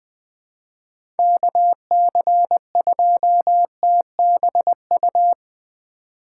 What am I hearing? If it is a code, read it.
Morse code, heard as a keyed tone, about 20 wpm, 700 Hz. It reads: KC2TBU
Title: USA Call Signs - 2x Character Spacing - 20wpm